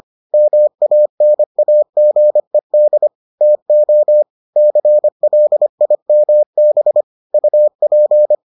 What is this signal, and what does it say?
Morse code, heard as a keyed tone, about 25 wpm, 600 Hz. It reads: MANAGED TO CLIMB UP